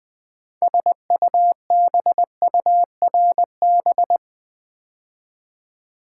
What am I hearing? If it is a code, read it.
Morse code, heard as a keyed tone, about 20 wpm, 700 Hz. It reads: SUBURB
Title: Build A Word - 2 Letters At A Time - 8-Letter Words - 20wpm